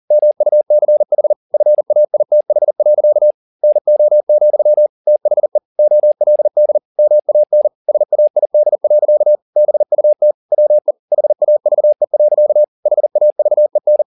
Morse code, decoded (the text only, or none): MUCH FAITH. NO, THE OLD MAN SAID. BUT WE HAVE. HAVEN